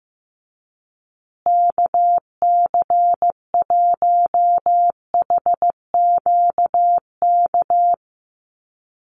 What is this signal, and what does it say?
Morse code, heard as a keyed tone, about 15 wpm, 700 Hz. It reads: KC1HQK
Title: USA Call Signs Repeated 3x 15wpm